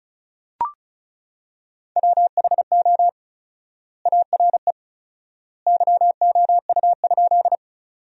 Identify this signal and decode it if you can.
Morse code, heard as a keyed tone, about 34 wpm, 700 Hz. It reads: WHO ARE YOU?
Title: 4x Word-Spacing - Sentences from Top 100 Words - 35wpm